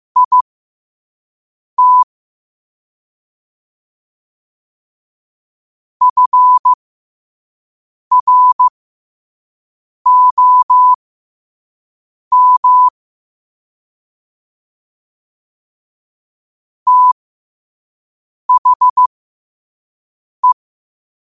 Morse code, heard as a keyed tone, about 15 wpm, spaced as for 5 wpm, 1000 Hz. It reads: IT FROM THE
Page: New quotes every day in morse code at 5 Words per minute.